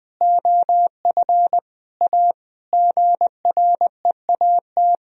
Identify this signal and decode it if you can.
Morse code, heard as a keyed tone, about 20 wpm, 700 Hz. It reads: OF A GREAT
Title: Sentences from Top 1000 Words - Review Part 2 - 20wpm